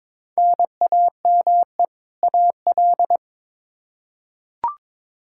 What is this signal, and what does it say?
Morse code, heard as a keyed tone, about 22 wpm, 700 Hz. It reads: NAME AL
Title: Practice QSO - Exchange 1 - Beginning - 22wpm